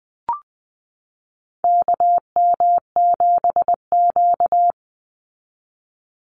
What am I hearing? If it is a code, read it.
Morse code, heard as a keyed tone, about 20 wpm, 700 Hz. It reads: KM7Q
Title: Call Signs Repeated 2x - 20wpm